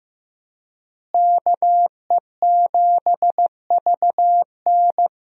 Morse code, heard as a keyed tone, about 15 wpm, 700 Hz. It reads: KE7VN
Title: USA Call Signs - 5 Characters - Repeated 2x - 15wpm